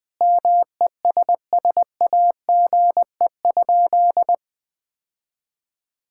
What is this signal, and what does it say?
Morse code, heard as a keyed tone, about 20 wpm, 700 Hz. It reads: MESSAGE?
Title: Sentences from Top 1600 Words 20wpm